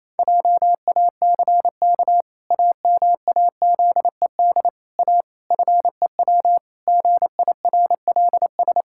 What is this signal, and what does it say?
Morse code, heard as a keyed tone, about 28 wpm, 700 Hz. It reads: JACK AMAZED A FEW GIRLS
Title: Pangram Challenge v2 - 28wpm